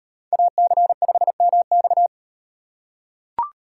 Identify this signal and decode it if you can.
Morse code, heard as a keyed tone, about 33 wpm, 700 Hz. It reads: AC5MX